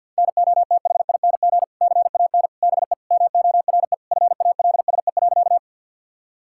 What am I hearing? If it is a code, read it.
Morse code, heard as a keyed tone, about 50 wpm, 700 Hz. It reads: NOTHING CAN BE MORE FALSE.
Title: Sentences from Top 1600 Words 50wpm